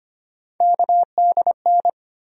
Morse code, heard as a keyed tone, about 25 wpm, 700 Hz. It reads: KDN
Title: USA Call Sign Postfixes - 3 Letters 25wpm